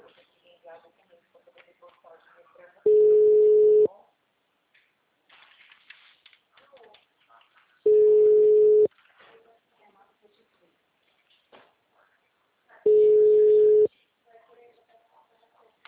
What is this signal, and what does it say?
Ringback tone (Europe, 425 Hz), with speech.